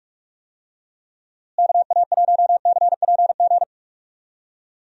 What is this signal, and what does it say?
Morse code, heard as a keyed tone, about 46 wpm, 700 Hz. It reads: KA1CPG
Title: USA Call Signs - 6 Characters - Repeated 2x - 45wpm